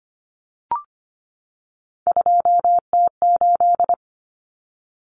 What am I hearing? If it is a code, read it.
Morse code, heard as a keyed tone, about 25 wpm, 700 Hz. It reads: 2T8
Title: Common Cut Numbers - 3-Digit 25wpm